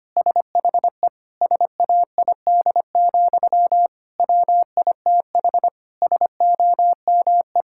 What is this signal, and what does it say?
Morse code, heard as a keyed tone, about 25 wpm, 700 Hz. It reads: SHE SAID, WITH SOME
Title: Sentences from Top 1600 Words 25wpm